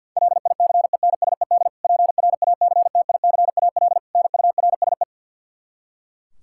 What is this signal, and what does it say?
Morse code, heard as a keyed tone, about 52 wpm, 700 Hz. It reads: LICENSED PRACTICAL NURSE
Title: Practice QSO - Exchange 3 - More Info - 50wpm